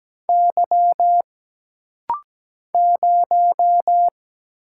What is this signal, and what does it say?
Morse code, heard as a keyed tone, about 17 wpm, 700 Hz. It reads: Y 0
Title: Single Letter-Number - Rapid-Fire 17wpm